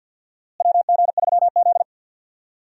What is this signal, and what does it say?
Morse code, heard as a keyed tone, about 51 wpm, 700 Hz. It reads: WG3Z
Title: USA Call Signs - 4 Characters - Repeated 2x - 50wpm